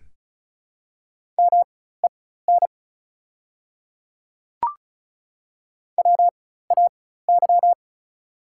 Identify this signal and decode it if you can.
Morse code, heard as a keyed tone, about 36 wpm, spaced as for 16 wpm, 690 Hz. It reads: MEN WAY
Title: 4x Character Spacing - Top 300 Words - 35wpm